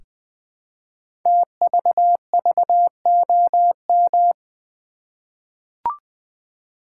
Morse code, heard as a keyed tone, about 20 wpm, 700 Hz. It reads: TVVOM
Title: Code Groups - 5 Letters - 20wpm